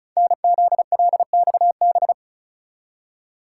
Morse code, heard as a keyed tone, about 36 wpm, 690 Hz. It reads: NZLXB
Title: Code Groups - 5 Letters - 35wpm